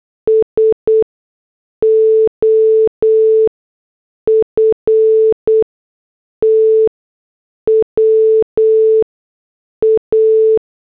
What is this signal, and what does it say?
Morse code, heard as a keyed tone, about 8 wpm, spaced as for 6 wpm, 440 Hz. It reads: SOFTWA
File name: morse / headlines